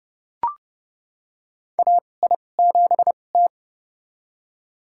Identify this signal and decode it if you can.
Morse code, heard as a keyed tone, about 30 wpm, spaced as for 22 wpm, 700 Hz. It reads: AI7T
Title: USA Call Signs - 2x Character Spacing - 30wpm